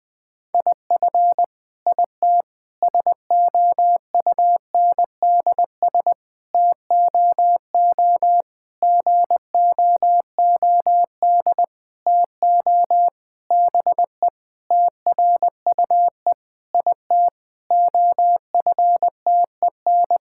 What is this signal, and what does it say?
Morse code, heard as a keyed tone, about 21 wpm, 700 Hz. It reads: IF IT SOUNDS TOO GOOD TO BE TRUE IT OFTEN
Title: Sentences with 9-Word Phrases - Spoken Beginning - 20wpm